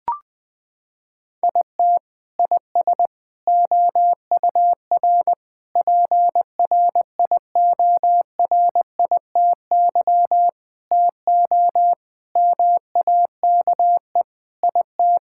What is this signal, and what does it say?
Morse code, heard as a keyed tone, about 21 wpm, 700 Hz. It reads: IT IS OUR PRIORITY TO MAKE IT
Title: Sentences from Top 1600 Words 20wpm